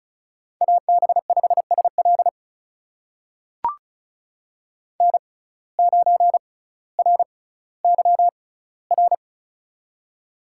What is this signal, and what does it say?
Morse code, heard as a keyed tone, about 36 wpm, spaced as for 18 wpm, 700 Hz. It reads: AB5SL N9RYR
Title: USA Call Signs - 5 Characters - 6x Character Spacing - 35wpm